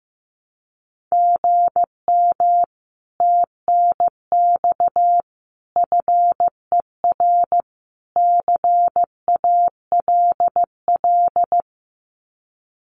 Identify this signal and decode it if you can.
Morse code, heard as a keyed tone, about 15 wpm, 700 Hz. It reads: GM TNX FER CALL
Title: Common QSO Phrases Encore 15wpm